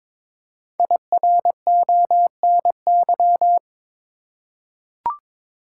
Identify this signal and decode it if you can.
Morse code, heard as a keyed tone, about 22 wpm, 700 Hz. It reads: IRONY
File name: 5 Letter Words - First 3 Letters Given - 22wpm